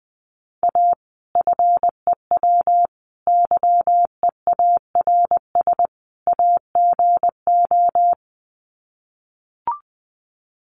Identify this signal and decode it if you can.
Morse code, heard as a keyed tone, about 20 wpm, 700 Hz. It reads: A FEW YEARS AGO
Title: Sets of 4 Words 20wpm